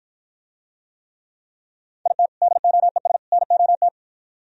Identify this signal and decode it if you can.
Morse code, heard as a keyed tone, about 49 wpm, 700 Hz. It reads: IT DOES NOT